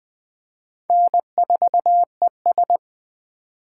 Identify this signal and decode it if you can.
Morse code, heard as a keyed tone, about 21 wpm, 700 Hz. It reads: N4ES
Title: Speed Racing - USA Call Signs 17wpm